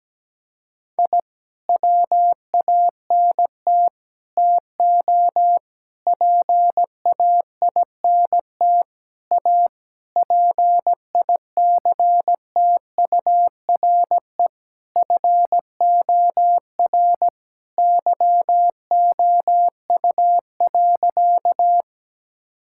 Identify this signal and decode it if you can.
Morse code, heard as a keyed tone, about 17 wpm, 700 Hz. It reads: I WANT TO PAINT A PICTURE FOR YOU.
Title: Sentences from Top 1000 Words - Review Part 6 - 17wpm